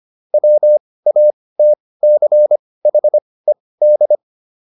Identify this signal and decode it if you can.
Morse code, heard as a keyed tone, about 25 wpm, spaced as for 18 wpm, 600 Hz. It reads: WATCHED
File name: ED7 18